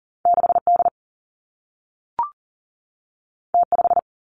Morse code, heard as a keyed tone, about 42 wpm, 700 Hz. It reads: 6 D T 5
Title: Two Challenging Letter-Number Combinations 40wpm